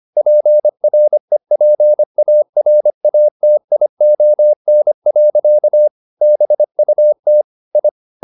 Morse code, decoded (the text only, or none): PREPARATION. BUT I